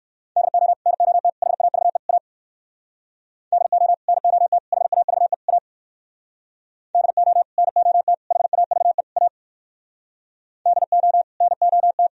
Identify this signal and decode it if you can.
Morse code, heard as a keyed tone, about 60 wpm, 700 Hz. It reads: DO NOT HAVE A DO NOT HAVE A DO NOT HAVE A DO NOT